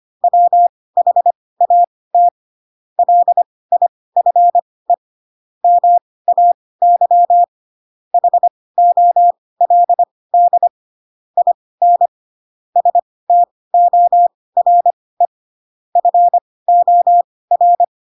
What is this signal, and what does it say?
Morse code, heard as a keyed tone, about 25 wpm, spaced as for 18 wpm, 700 Hz. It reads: WHAT LIFE MAY HOLD IN STORE FOR
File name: Pinocchio / Pinocchio ch7 12 25 ik1hge